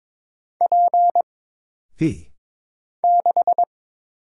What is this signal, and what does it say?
Morse code, heard as a keyed tone, about 22 wpm, 700 Hz. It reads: P 6
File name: Single Character (Letter:Number:Punctuation:Prosign) - Mind-Melt - 22wpm